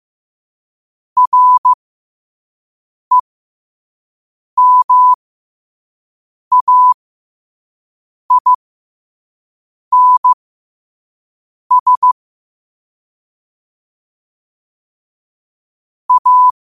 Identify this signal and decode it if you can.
Morse code, heard as a keyed tone, about 15 wpm, spaced as for 5 wpm, 1000 Hz. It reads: REMAINS A
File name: qotdfiles / 5wpm